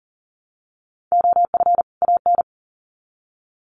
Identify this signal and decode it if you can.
Morse code, heard as a keyed tone, about 40 wpm, 700 Hz. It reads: OF AN
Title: Sets of 2 Words 40wpm